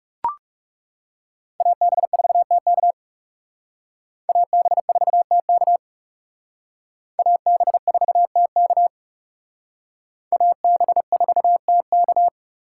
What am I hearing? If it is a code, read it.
Morse code, heard as a keyed tone, about 35 wpm, 700 Hz. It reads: AB4TK AB4TK AB4TK AB4TK
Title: Speed Racing - USA 2x2 Call Signs 30wpm